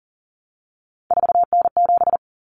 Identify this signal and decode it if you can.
Morse code, heard as a keyed tone, about 40 wpm, 700 Hz. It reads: 4N7
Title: Common Cut Numbers - 3-Digit 40wpm